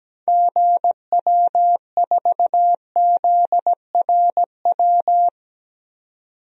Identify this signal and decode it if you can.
Morse code, heard as a keyed tone, about 17 wpm, 700 Hz. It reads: GW4ZRW
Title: DXCC-100 Call Signs - 6 Characters - 17wpm